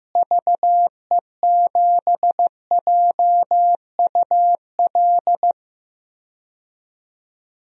Morse code, heard as a keyed tone, about 15 wpm, 700 Hz. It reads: VE7JUL
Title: DXCC-100 Call Signs - 6 Characters - Repeated 3x - 15wpm